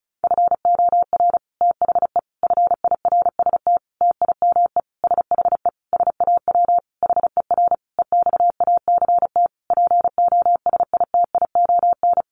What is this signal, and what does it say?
Morse code, heard as a keyed tone, about 34 wpm, 700 Hz. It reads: FOR THE FIRST TIME SHE SAW HER EXACT POSITION
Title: Sentences from Top 1300 Words 35wpm